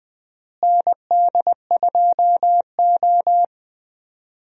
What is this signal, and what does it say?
Morse code, heard as a keyed tone, about 21 wpm, 700 Hz. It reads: ND2O
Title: USA Call Signs - 4 Characters - Repeated 2x - 20wpm